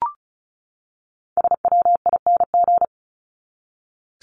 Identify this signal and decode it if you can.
Morse code, heard as a keyed tone, about 34 wpm, 700 Hz. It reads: SWING
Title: Top 1751-2000 Words 35wpm